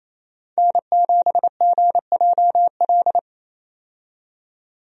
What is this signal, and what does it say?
Morse code, heard as a keyed tone, about 28 wpm, 700 Hz. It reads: N7GJL
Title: USA Call Signs - 5 Characters - 28wpm